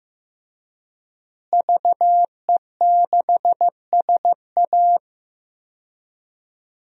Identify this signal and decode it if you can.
Morse code, heard as a keyed tone, about 15 wpm, 700 Hz. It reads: VE6SA
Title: DXCC-100 Call Signs - 5 Characters - Repeated 3x - 15wpm